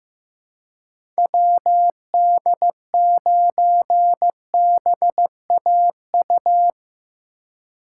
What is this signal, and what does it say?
Morse code, heard as a keyed tone, about 15 wpm, 700 Hz. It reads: WD9BAU
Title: USA Call Signs - 6 Characters - 6x Character Spacing - 15wpm